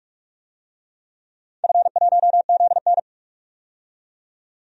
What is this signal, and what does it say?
Morse code, heard as a keyed tone, about 42 wpm, 700 Hz. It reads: F1ZN
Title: DXCC-100 Call Signs - 4 Characters - 45wpm